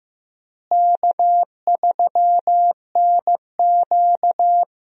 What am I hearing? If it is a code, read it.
Morse code, heard as a keyed tone, about 15 wpm, 700 Hz. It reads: K3NQ
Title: USA Call Signs - 4 Characters - Repeated 2x - 15wpm